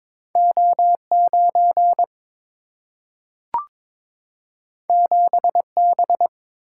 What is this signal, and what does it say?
Morse code, heard as a keyed tone, about 22 wpm, 700 Hz. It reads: O 9 7 B
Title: Single-Letter Single-Number 22wpm